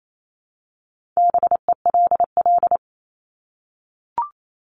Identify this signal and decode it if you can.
Morse code, heard as a keyed tone, about 28 wpm, 700 Hz. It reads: BELL
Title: Top 1751-2000 Words 28wpm